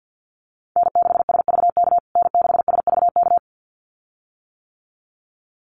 Morse code, heard as a keyed tone, about 50 wpm, 700 Hz. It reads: N6SVK N6SVK
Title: USA Call Signs Repeated 2x 50wpm